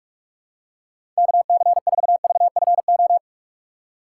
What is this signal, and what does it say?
Morse code, heard as a keyed tone, about 46 wpm, 700 Hz. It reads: KC4VFO
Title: USA Call Signs - 6 Characters - 3x Character Spacing - 45wpm